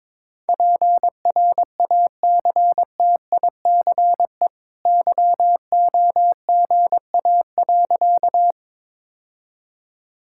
Morse code, heard as a keyed tone, about 22 wpm, 690 Hz. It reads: PRACTICE YOGA.
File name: practice / 22wpm / Easy Sentences with Hints - 22wpm